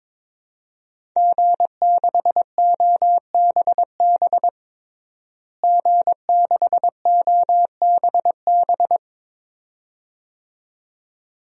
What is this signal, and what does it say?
Morse code, heard as a keyed tone, about 22 wpm, 700 Hz. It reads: G6OBB G6OBB
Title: DXCC-100 Call Signs - 5 Characters - Repeated 2x - 22wpm